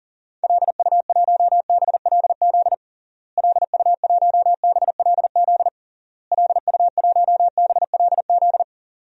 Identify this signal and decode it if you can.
Morse code, heard as a keyed tone, about 40 wpm, 700 Hz. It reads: LU1BLZ LU1BLZ LU1BLZ
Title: DXCC-100 Call Signs - 6 Characters - Repeated 3x - 40wpm